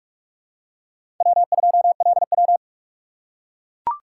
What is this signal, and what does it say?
Morse code, heard as a keyed tone, about 44 wpm, 700 Hz. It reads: W2LW
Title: USA Call Signs - 4 Characters - 3x Character Spacing - 45wpm